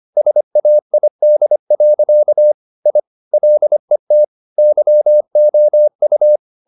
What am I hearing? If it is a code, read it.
Morse code, heard as a keyed tone, about 25 wpm, 600 Hz. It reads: SAID. I LET YOU
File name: the old man and the sea part 1 25wpm